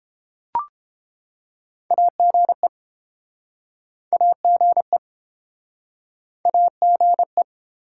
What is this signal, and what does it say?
Morse code, heard as a keyed tone, about 33 wpm, 700 Hz. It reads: AGE AGE AGE